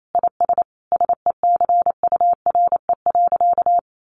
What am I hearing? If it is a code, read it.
Morse code, heard as a keyed tone, about 28 wpm, 700 Hz. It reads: IS SECURE.
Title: Sentences from Top 1200 Words 28wpm